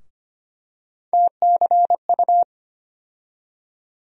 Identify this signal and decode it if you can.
Morse code, heard as a keyed tone, about 25 wpm, 700 Hz. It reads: TCU
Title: USA Call Sign Postfixes - 3 Letters 25wpm